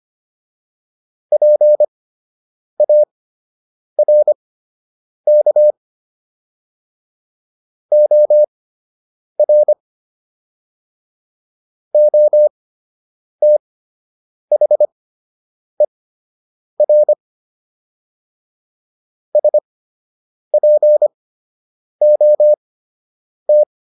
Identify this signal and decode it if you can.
Morse code, heard as a keyed tone, about 25 wpm, spaced as for 8 wpm, 600 Hz. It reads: PARK OR OTHER SPOT